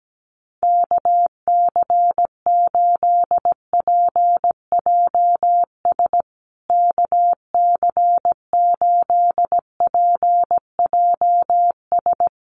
Morse code, heard as a keyed tone, about 17 wpm, 700 Hz. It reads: KC8PJS KC8PJS
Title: USA Call Signs Repeated 3x 17wpm